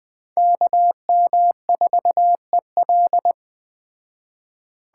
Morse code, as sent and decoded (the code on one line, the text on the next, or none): -.- -- ....- . .-..
KM4EL